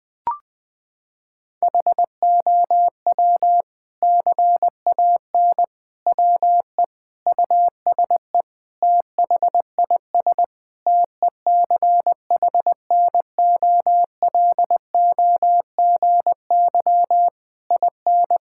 Morse code, heard as a keyed tone, about 21 wpm, 700 Hz. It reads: HOW CAN WE USE THIS TECHNOLOGY IN
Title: Sentences from Top 1000 Words - Review Part 5 - 20wpm